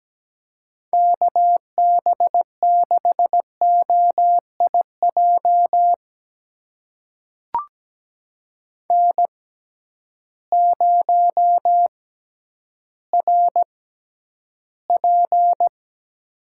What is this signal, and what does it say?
Morse code, heard as a keyed tone, about 17 wpm, spaced as for 10 wpm, 700 Hz. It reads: KB6OIJ N0RP
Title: USA Call Signs - 6x Character Spacing - 17wpm